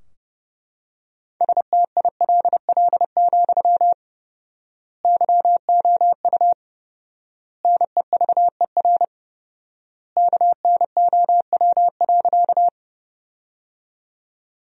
Morse code, heard as a keyed tone, about 32 wpm, 700 Hz. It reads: STILL, YOU NEVER KNOW.
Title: 4x Word-Spacing - Sentences from Top 300 Words - 30wpm